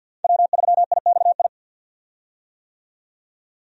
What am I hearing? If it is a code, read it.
Morse code, heard as a keyed tone, about 47 wpm, 700 Hz. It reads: W3IXI
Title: USA Call Signs - 5 Characters - 50wpm